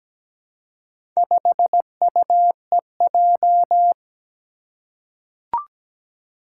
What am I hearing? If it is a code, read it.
Morse code, heard as a keyed tone, about 17 wpm, 700 Hz. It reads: 5UEJ
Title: Four Challenging Letter-Number Combinations 17wpm